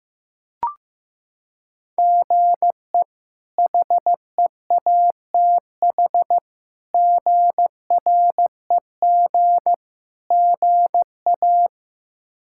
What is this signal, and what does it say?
Morse code, heard as a keyed tone, about 15 wpm, 700 Hz. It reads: GE HEATH GREG GA
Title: K1USN SST QSO Practice - 15wpm